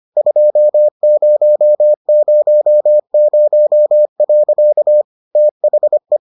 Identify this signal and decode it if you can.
Morse code, heard as a keyed tone, about 25 wpm, 600 Hz. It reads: 2000. THE